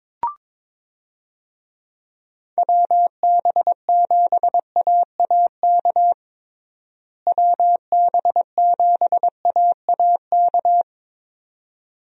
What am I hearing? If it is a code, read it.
Morse code, heard as a keyed tone, about 21 wpm, 700 Hz. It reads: WB7AAK WB7AAK
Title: USA Call Signs - 6 Characters - Repeated 2x - 22wpm